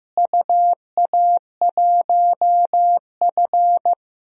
Morse code, heard as a keyed tone, about 15 wpm, 700 Hz. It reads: UA1F
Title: DXCC-100 Call Signs - 4 Characters - Repeated 3x - 15wpm